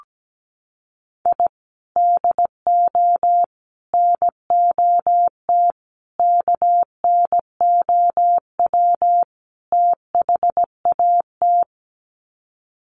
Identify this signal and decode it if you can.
Morse code, heard as a keyed tone, about 17 wpm, 700 Hz. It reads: I DO NOT KNOW THAT
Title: Sets of 5 Words 17wpm